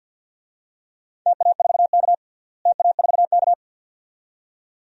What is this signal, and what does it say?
Morse code, heard as a keyed tone, about 49 wpm, 700 Hz. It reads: TA4K TA4K
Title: DXCC-100 Call Signs - 4 Characters - Repeated 2x - 50wpm